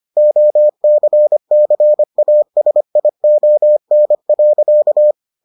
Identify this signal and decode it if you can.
Morse code, heard as a keyed tone, about 25 wpm, 600 Hz. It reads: OCCASION.